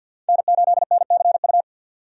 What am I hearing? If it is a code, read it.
Morse code, heard as a keyed tone, about 51 wpm, 700 Hz. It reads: N8NCU
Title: USA Call Signs - 5 Characters - 3x Character Spacing - 50wpm